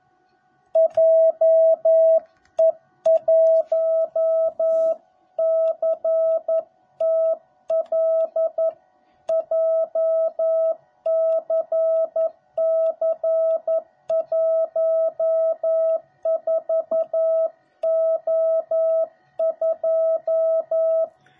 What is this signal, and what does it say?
Morse code, heard as a keyed tone, about 11 wpm, 660 Hz. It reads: JE1CTL JCC14O2